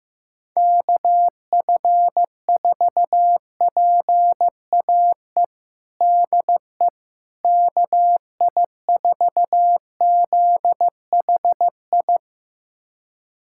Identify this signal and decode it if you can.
Morse code, heard as a keyed tone, about 15 wpm, 700 Hz. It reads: KF4PAE DE KI4ZHI
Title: USA Callsign DE Callsign - 6 Characters - 15wpm